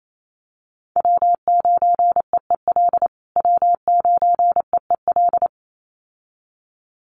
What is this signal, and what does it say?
Morse code, heard as a keyed tone, about 28 wpm, 700 Hz. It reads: W9EEL W9EEL
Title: USA Call Signs Repeated 2x 28wpm